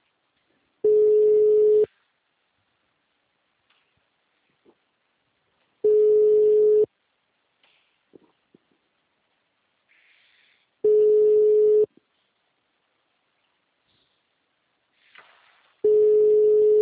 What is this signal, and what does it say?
Ringback tone (Europe, 425 Hz).
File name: out- -2015- - - .14